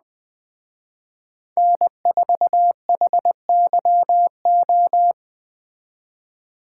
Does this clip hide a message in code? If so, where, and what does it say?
Yes: Morse code from 1.6 to 5.1 s, heard as a keyed tone, about 20 wpm, 700 Hz. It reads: N4HYO